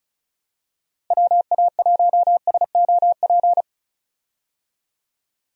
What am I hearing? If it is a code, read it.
Morse code, heard as a keyed tone, about 36 wpm, 700 Hz. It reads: WA1SOP
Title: USA Call Signs - 6 Characters - Repeated 2x - 35wpm